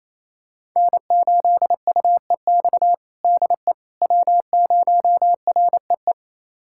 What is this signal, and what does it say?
Morse code, heard as a keyed tone, about 28 wpm, 700 Hz. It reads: N8UEX DE W0REE
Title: USA Callsign DE Callsign - 5 Characters - 28wpm